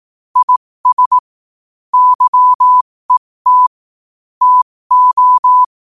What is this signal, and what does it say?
Morse code, heard as a keyed tone, about 18 wpm, spaced as for 15 wpm, 1000 Hz. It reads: IS YET TO
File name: qotdfiles / 15wpm